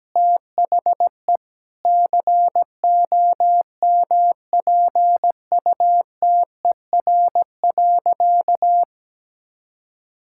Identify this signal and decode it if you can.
Morse code, heard as a keyed tone, about 17 wpm, 700 Hz. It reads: THE COMPUTER.
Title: Sentences from Top 1000 Words - Review Part 6 - 17wpm